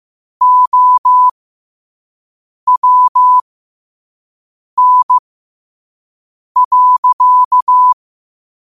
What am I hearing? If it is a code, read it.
Morse code, heard as a keyed tone, about 15 wpm, spaced as for 5 wpm, 1000 Hz. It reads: OWN.
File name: qotdfiles / 5wpm